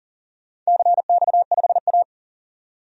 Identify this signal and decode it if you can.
Morse code, heard as a keyed tone, about 40 wpm, 700 Hz. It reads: CX5A
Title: DXCC-100 Call Signs - 4 Characters - 40wpm